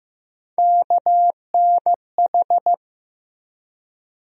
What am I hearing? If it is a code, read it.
Morse code, heard as a keyed tone, about 15 wpm, 700 Hz. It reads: KNH
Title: USA Call Sign Postfixes - 3 Letters 15wpm